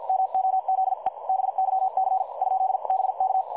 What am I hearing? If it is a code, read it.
Morse code, heard as a keyed tone, about 50 wpm, spaced as for 44 wpm, 760 Hz. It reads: MOZE BYC ?BK
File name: dlazaawansowanych / nagraniazpasma / websdr recording 3565.0kHz